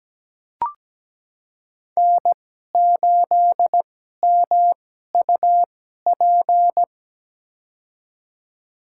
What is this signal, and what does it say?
Morse code, heard as a keyed tone, about 17 wpm, spaced as for 12 wpm, 700 Hz. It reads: N8MUP
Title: USA Call Signs - 5 Characters - 2x Character Spacing - 17wpm